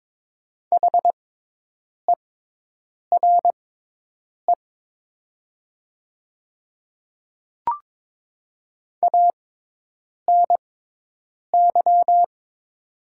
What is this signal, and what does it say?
Morse code, heard as a keyed tone, about 22 wpm, spaced as for 8 wpm, 700 Hz. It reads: HERE ANY
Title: 6x Character Spacing - Top 200 Words - 22wpm